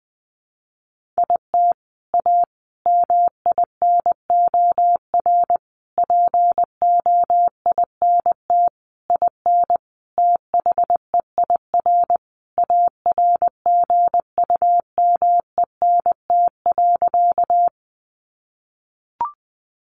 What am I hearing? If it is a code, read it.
Morse code, heard as a keyed tone, about 20 wpm, 700 Hz. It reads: IT A MINOR POINT IN THEIR ARGUMENT.
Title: Sentences from Top 1500 Words 20wpm